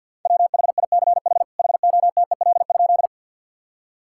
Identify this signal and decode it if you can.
Morse code, heard as a keyed tone, about 51 wpm, 700 Hz. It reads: WHICH HOTEL?
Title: Sentences from Top 1600 Words 50wpm